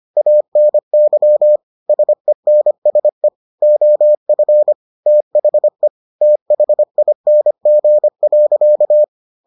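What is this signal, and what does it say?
Morse code, heard as a keyed tone, about 25 wpm, 600 Hz. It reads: ANY SENSE OF THE THING.
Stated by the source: Morse edition of the Foundations of Amateur Radio episodes.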